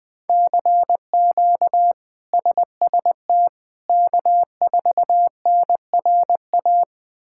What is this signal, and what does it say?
Morse code, heard as a keyed tone, about 20 wpm, 700 Hz. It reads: CQ SST K4NRA
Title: K1USN SST QSO Practice - 20wpm